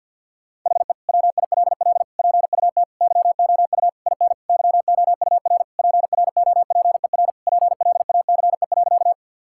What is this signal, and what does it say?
Morse code, heard as a keyed tone, about 50 wpm, 700 Hz. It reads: HE WILL PUT YOU IN YOUR PROPER PLACE.
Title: Sentences from Top 1000 Words - Review Part 5 - 50wpm